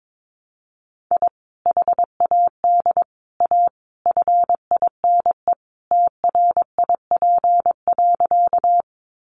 Morse code, heard as a keyed tone, about 22 wpm, 700 Hz. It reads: I HAD A FINE TRIP.
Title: Sentences from Top 1300 Words 22wpm